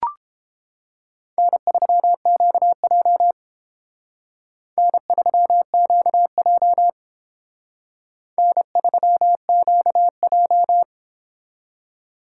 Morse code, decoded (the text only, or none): N3QJ N3QJ N3QJ